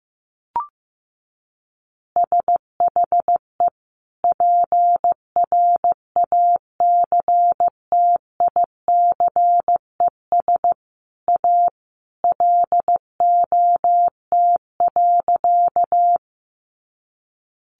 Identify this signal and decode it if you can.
Morse code, heard as a keyed tone, about 15 wpm, 700 Hz. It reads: SHE PRACTICES A LOT.
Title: Additional Easy Phrases 15wpm